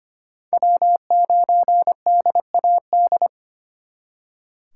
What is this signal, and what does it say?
Morse code, heard as a keyed tone, about 33 wpm, 700 Hz. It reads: W9DAD